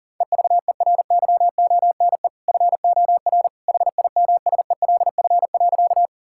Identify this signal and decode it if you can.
Morse code, heard as a keyed tone, about 40 wpm, 700 Hz. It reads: EVERYONE FOR HIMSELF.
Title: Sentences from Top 1600 Words 40wpm